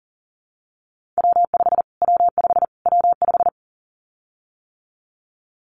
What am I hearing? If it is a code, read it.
Morse code, heard as a keyed tone, about 40 wpm, 700 Hz. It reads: W5 W5 W5
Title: Callsign Prefixes for Top 25 DXCC - Repeated 3x - 40wpm